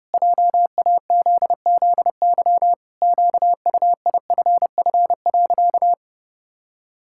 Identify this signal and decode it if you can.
Morse code, heard as a keyed tone, about 32 wpm, 700 Hz. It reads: JAZZY QUIFF.
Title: Pangram Challenge v2 - 30wpm